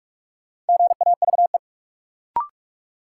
Morse code, heard as a keyed tone, about 46 wpm, 700 Hz. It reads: GAVE
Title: Sentences from Top 500 Words - New Word Drill - 45wpm